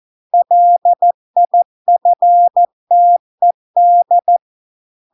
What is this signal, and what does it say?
Morse code, heard as a keyed tone, about 14 wpm, 700 Hz. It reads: LIFTED